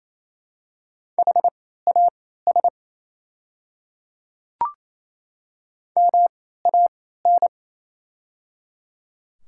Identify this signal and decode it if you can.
Morse code, heard as a keyed tone, about 28 wpm, spaced as for 16 wpm, 700 Hz. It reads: HAS MAN
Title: 3x Character Spacing - Top 200 Words - 28wpm